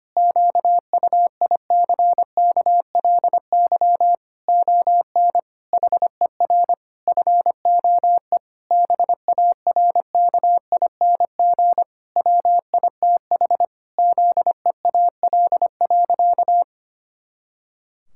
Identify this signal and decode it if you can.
Morse code, heard as a keyed tone, about 25 wpm, 700 Hz. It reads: QUICKLY ON HER FOE BARKING WITH ZEAL.
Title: Pangram Challenge v2 - 25wpm